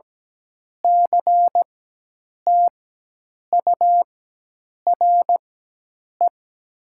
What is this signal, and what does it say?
Morse code, heard as a keyed tone, about 17 wpm, spaced as for 8 wpm, 700 Hz. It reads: CTURE
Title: 4x Character Spacing - Top 500 Words - 17wpm